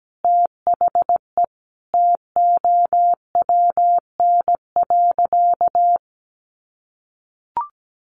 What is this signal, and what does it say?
Morse code, heard as a keyed tone, about 17 wpm, 700 Hz. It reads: THE TOWN.
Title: Sentences from Top 1400 Words 17wpm